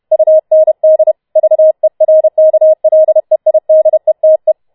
Morse code, heard as a keyed tone, about 31 wpm, 630 Hz. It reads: UND VERKLEIDETE